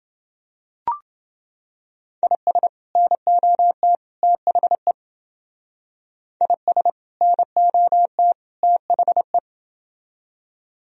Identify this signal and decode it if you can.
Morse code, heard as a keyed tone, about 28 wpm, 700 Hz. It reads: IS NOT THE IS NOT THE